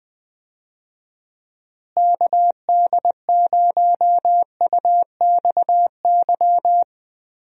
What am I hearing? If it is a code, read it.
Morse code, heard as a keyed tone, about 20 wpm, 700 Hz. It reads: KD0UXY
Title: USA Call Signs - 6 Characters - Repeated 3x - 20wpm